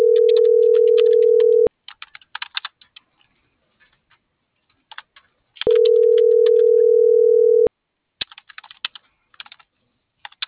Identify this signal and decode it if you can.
Ringback tone (North America), 440 + 480 Hz.
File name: CENTRAL MULTIMODAL / out viridiana.juarez